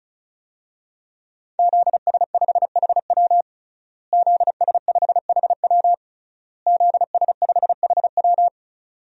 Morse code, heard as a keyed tone, about 36 wpm, 700 Hz. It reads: ZS5HW ZS5HW ZS5HW
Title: DXCC-100 Call Signs - 5 Characters - Repeated 3x - 35wpm